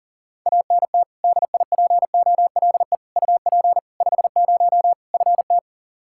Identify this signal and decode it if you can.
Morse code, heard as a keyed tone, about 40 wpm, 700 Hz. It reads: ANT DIPOLE UP 50 FT
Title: K1USN SST QSO Practice 40wpm